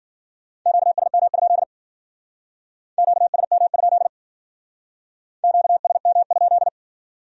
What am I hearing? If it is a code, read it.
Morse code, heard as a keyed tone, about 51 wpm, 700 Hz. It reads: QSM? QSM? QSM?